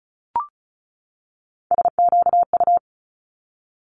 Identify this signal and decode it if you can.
Morse code, heard as a keyed tone, about 36 wpm, 700 Hz. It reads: SQU
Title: Multi-letter Phonemes 35wpm